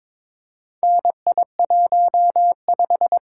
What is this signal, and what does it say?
Morse code, heard as a keyed tone, about 22 wpm, 700 Hz. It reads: NI15
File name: practice / 22wpm / Four Letter-Number Combinations - 22wpm